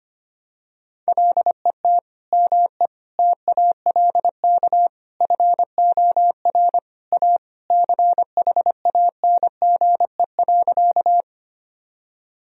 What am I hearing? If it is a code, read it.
Morse code, heard as a keyed tone, about 25 wpm, 700 Hz. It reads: LET ME TALK FOR A CHANGE.
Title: Sentences from Top 1000 Words - Review Part 1 - 25wpm